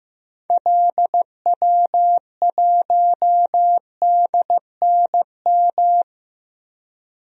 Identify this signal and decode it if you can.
Morse code, heard as a keyed tone, about 15 wpm, 700 Hz. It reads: LW1DNM
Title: DXCC-100 Call Signs - 6 Characters - 15wpm